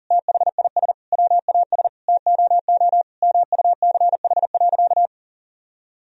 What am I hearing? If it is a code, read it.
Morse code, heard as a keyed tone, about 40 wpm, 700 Hz. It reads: THIS WAS TOO MUCH.
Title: Sentences from Top 1000 Words - Review Part 1 - 40wpm